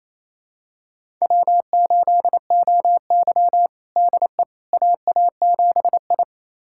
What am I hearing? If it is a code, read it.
Morse code, heard as a keyed tone, about 28 wpm, 700 Hz. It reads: W8OY DE AA7I
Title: USA Callsign DE Callsign - 4 Characters - 28wpm